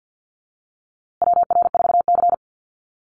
Morse code, heard as a keyed tone, about 50 wpm, 700 Hz. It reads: WR4C
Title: USA Call Signs Repeated 3x 50wpm